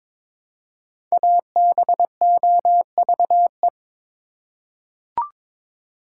Morse code, heard as a keyed tone, about 23 wpm, 700 Hz. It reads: ABOVE
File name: 5 Letter Words - First 3 Letters Given - 22wpm